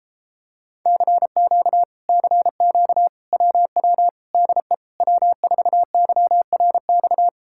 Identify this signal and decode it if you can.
Morse code, heard as a keyed tone, about 33 wpm, 700 Hz. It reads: CQ CQ WW DE W4YRX